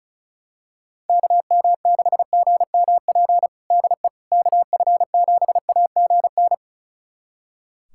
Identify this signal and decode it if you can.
Morse code, heard as a keyed tone, about 36 wpm, 700 Hz. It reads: KM6GMP DE KF7AGN
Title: USA Callsign DE Callsign - 6 Characters - 35wpm